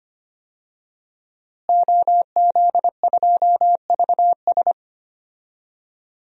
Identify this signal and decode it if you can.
Morse code, heard as a keyed tone, about 25 wpm, 700 Hz. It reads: OZ2VS
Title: DXCC-100 Call Signs - 5 Characters - Repeated 2x - 25wpm